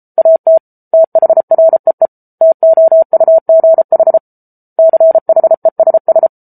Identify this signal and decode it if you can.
Morse code, heard as a keyed tone, about 34 wpm, 650 Hz. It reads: AT THREE TOUGH CHESS
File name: Book Two / 07-Chapter 7 - 33 WPM